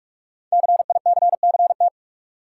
Morse code, heard as a keyed tone, about 44 wpm, 700 Hz. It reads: CICCT
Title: Code Groups - 5 Letters - 45wpm